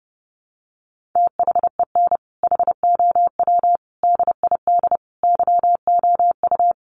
Morse code, heard as a keyed tone, about 30 wpm, 700 Hz. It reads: THEN HOW DID YOU
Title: Sentences from Top 1400 Words 30wpm